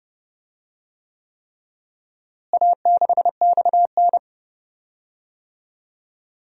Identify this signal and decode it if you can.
Morse code, heard as a keyed tone, about 30 wpm, 700 Hz. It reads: A6XN
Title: DXCC-100 Call Signs - 4 Characters - Repeated 3x - 30wpm